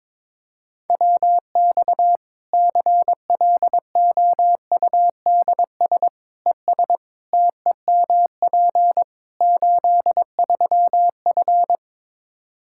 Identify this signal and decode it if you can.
Morse code, heard as a keyed tone, about 22 wpm, 700 Hz. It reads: WX CLOUDS ES TEMP 83F
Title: K1USN SST QSO Practice 22wpm